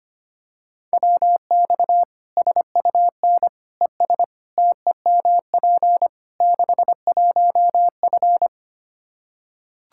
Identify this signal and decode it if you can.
Morse code, heard as a keyed tone, about 25 wpm, 700 Hz. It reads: WX SUN ES TEMP 61F
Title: Practice QSO - Exchange 2 - Info - 25wpm